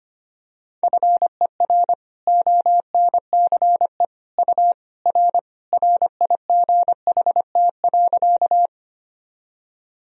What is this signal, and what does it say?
Morse code, heard as a keyed tone, about 25 wpm, 700 Hz. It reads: FER ONCE U R RIGHT.
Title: Sentences from Top 300 Words and Abbreviations 25wpm